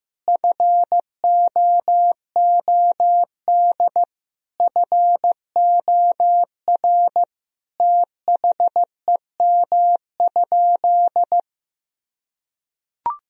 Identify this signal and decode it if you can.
Morse code, heard as a keyed tone, about 15 wpm, 700 Hz. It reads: FOOD FOR THEM?
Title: Sentences from Top 1000 Words - Review Part 1 - 15wpm